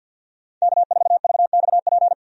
Morse code, heard as a keyed tone, about 47 wpm, 700 Hz. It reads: K4VXP
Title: USA Call Signs - 5 Characters - 50wpm